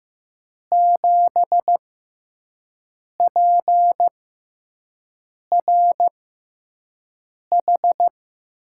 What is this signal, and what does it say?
Morse code, heard as a keyed tone, about 15 wpm, spaced as for 5 wpm, 700 Hz. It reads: 7PRH
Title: USA Call Signs - 6 Characters - 6x Character Spacing - 15wpm